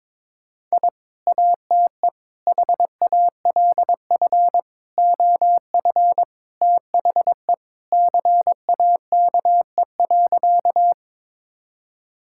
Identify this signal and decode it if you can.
Morse code, heard as a keyed tone, about 23 wpm, 700 Hz. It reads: I ATE HALF OF THE CAKE.
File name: practice / 22wpm / Sentences with 3-Word Phrases - 22wpm